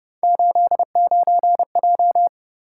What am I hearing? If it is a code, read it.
Morse code, heard as a keyed tone, about 33 wpm, 700 Hz. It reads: 89J